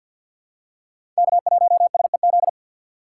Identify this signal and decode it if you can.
Morse code, heard as a keyed tone, about 48 wpm, 690 Hz. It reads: K1SEZ
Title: USA Call Signs - 5 Characters - 4x Character Spacing - 50wpm